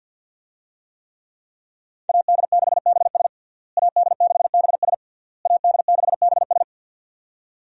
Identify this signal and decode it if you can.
Morse code, heard as a keyed tone, about 49 wpm, 700 Hz. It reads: AD6BS AD6BS AD6BS
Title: USA Call Signs - 5 Characters - Repeated 3x - 50wpm